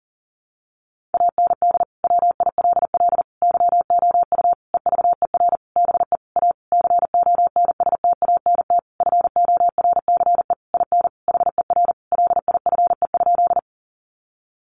Morse code, heard as a keyed tone, about 40 wpm, 700 Hz. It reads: AND WILL YOU EVER BE A CONSTANT FORCE IN HER LIFE?
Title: Sentences from Top 1500 Words 40wpm